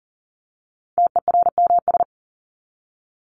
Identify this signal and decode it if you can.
Morse code, heard as a keyed tone, about 40 wpm, 700 Hz. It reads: TERMS
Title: ES:S Suffix - Top 100 Words with Base Words 40wpm